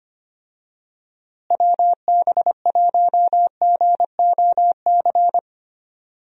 Morse code, heard as a keyed tone, about 25 wpm, 700 Hz. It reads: WB1GOC
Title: USA Call Signs - 6 Characters - Repeated 3x - 25wpm